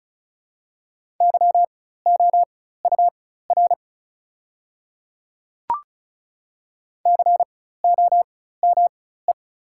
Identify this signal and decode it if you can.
Morse code, heard as a keyed tone, about 34 wpm, spaced as for 16 wpm, 700 Hz. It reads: YOUR COME
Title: 4x Character Spacing - Top 200 Words - 35wpm